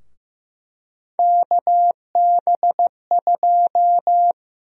Morse code, heard as a keyed tone, about 15 wpm, 700 Hz. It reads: KB2
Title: USA Call Sign Prefixes - 15wpm